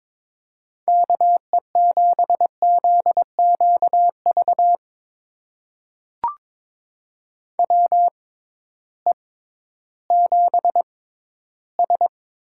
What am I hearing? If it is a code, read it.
Morse code, heard as a keyed tone, about 22 wpm, spaced as for 13 wpm, 700 Hz. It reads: KE7ZQV WE7S
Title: USA Call Signs - 6x Character Spacing - 22wpm